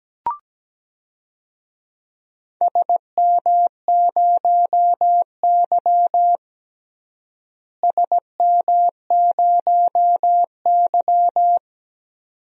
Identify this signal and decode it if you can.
Morse code, heard as a keyed tone, about 17 wpm, 700 Hz. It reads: SM0Y SM0Y
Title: DXCC-100 Call Signs - 4 Characters - Repeated 2x - 17wpm